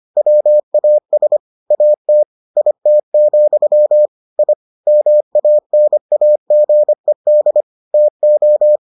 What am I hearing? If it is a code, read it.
Morse code, heard as a keyed tone, about 25 wpm, 600 Hz. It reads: WAS AT IT, I MANAGED TO